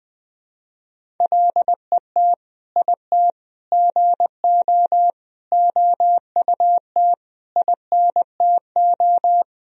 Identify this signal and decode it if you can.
Morse code, heard as a keyed tone, about 21 wpm, 700 Hz. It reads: LET IT GO OUT INTO
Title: Sentences from Top 1000 Words - Review Part 1 - 20wpm